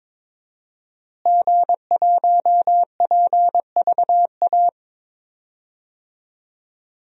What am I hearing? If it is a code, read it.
Morse code, heard as a keyed tone, about 22 wpm, 700 Hz. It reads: G1PVA
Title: DXCC-100 Call Signs - 5 Characters - Repeated 2x - 22wpm